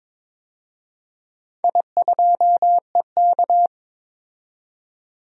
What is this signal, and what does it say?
Morse code, heard as a keyed tone, about 22 wpm, 700 Hz. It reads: I2EK
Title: DXCC-100 Call Signs - 4 Characters - 22wpm